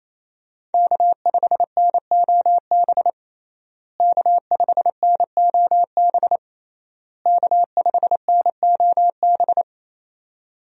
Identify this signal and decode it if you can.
Morse code, heard as a keyed tone, about 27 wpm, 700 Hz. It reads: K5NOB K5NOB K5NOB
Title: USA Call Signs - 5 Characters - Repeated 3x - 28wpm